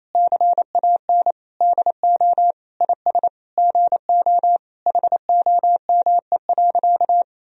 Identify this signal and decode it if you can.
Morse code, heard as a keyed tone, about 28 wpm, 700 Hz. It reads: CAN DO IS GO HOME.
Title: Sentences from Top 200 Words and Abbreviations 28wpm